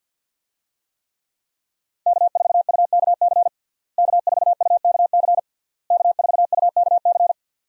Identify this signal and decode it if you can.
Morse code, heard as a keyed tone, about 52 wpm, 700 Hz. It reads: K4UKC K4UKC K4UKC
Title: USA Call Signs - 5 Characters - Repeated 3x - 50wpm